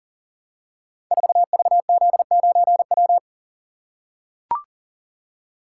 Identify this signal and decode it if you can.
Morse code, heard as a keyed tone, about 40 wpm, 690 Hz. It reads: 4VZ9W
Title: Code Groups - 5 Letter:Numbers - 40wpm